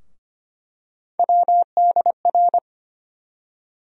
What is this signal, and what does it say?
Morse code, heard as a keyed tone, about 25 wpm, 700 Hz. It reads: WDR
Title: USA Call Sign Postfixes - 3 Letters 25wpm